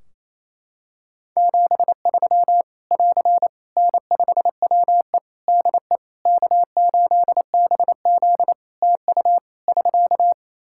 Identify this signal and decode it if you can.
Morse code, heard as a keyed tone, about 28 wpm, 700 Hz. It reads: 73 <AR> N5WE DE K8BZ TU <SK>
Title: K1USN SST QSO Practice 28wpm